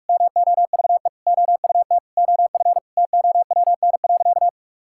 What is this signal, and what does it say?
Morse code, heard as a keyed tone, about 46 wpm, 700 Hz. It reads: MOVE OUT OF TOWN.
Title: Sentences from Top 1000 Words - Review Part 6 - 45wpm